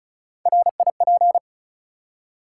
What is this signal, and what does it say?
Morse code, heard as a keyed tone, about 36 wpm, 700 Hz. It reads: RIP
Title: Three Letter Words - 35wpm